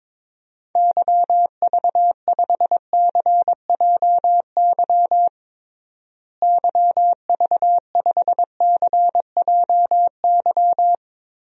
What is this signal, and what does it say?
Morse code, heard as a keyed tone, about 21 wpm, 700 Hz. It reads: YV5CJY YV5CJY
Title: DXCC-100 Call Signs - 6 Characters - Repeated 2x - 22wpm